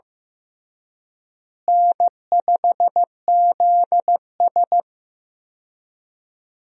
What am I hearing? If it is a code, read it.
Morse code, heard as a keyed tone, about 16 wpm, 700 Hz. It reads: N5ZS
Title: USA Call Signs - 4 Characters - Repeated 2x - 15wpm